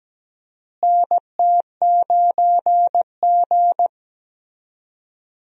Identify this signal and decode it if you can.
Morse code, heard as a keyed tone, about 17 wpm, 700 Hz. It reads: NT9G
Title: USA Call Signs - 4 Characters - Repeated 2x - 17wpm